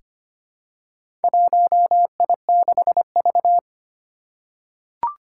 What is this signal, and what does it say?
Morse code, heard as a keyed tone, about 25 wpm, 700 Hz. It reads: 1I6V
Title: Four Challenging Letter-Number Combinations 25wpm